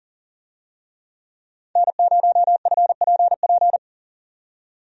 Morse code, heard as a keyed tone, about 40 wpm, 700 Hz. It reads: N0FPP
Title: USA Call Signs - 5 Characters - 40wpm